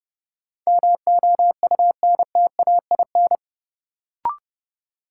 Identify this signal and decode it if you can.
Morse code, heard as a keyed tone, about 29 wpm, 700 Hz. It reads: MOUNTAIN
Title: Sentences from Top 1300 Words - New Word Drill Alt - 30wpm